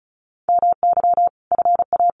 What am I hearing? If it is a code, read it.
Morse code, heard as a keyed tone, about 34 wpm, 700 Hz. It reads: MY FA
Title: Sentences from Top 1500 Words 35wpm